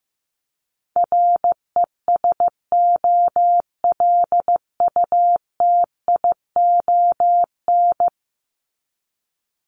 Morse code, heard as a keyed tone, about 14 wpm, 700 Hz. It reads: RESOLUTION
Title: Top 1501-1750 Words 15wpm